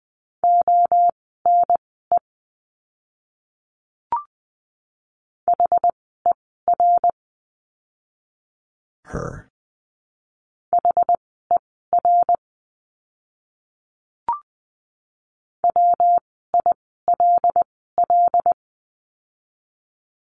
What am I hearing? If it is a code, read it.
Morse code, heard as a keyed tone, about 20 wpm, spaced as for 15 wpm, 700 Hz. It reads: ONE HER HER WILL
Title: 244 Lesson 39f - Letter z - Words 20wpm